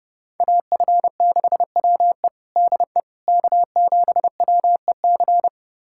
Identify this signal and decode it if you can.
Morse code, heard as a keyed tone, about 29 wpm, 700 Hz. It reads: AF6WE DE K7WEC
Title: USA Callsign DE Callsign - 5 Characters - 30wpm